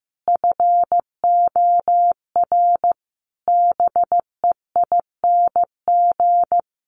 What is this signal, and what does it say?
Morse code, heard as a keyed tone, about 15 wpm, 700 Hz. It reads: FOR BEING
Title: Sets of 5 Words 15wpm